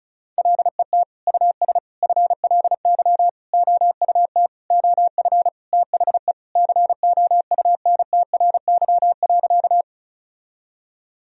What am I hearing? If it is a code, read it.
Morse code, heard as a keyed tone, about 36 wpm, 700 Hz. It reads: LET US FLY OUT OF THE COUNTRY.
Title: Sentences from Top 1000 Words - Review Part 2 - 35wpm